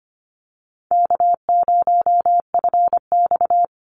Morse code, heard as a keyed tone, about 25 wpm, 700 Hz. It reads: K0FX
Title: Call Signs - 25wpm